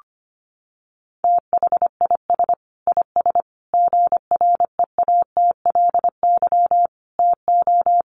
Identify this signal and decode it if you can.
Morse code, heard as a keyed tone, about 25 wpm, 700 Hz. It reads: THIS IS GREATLY TO
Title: Sentences from Top 1400 Words 25wpm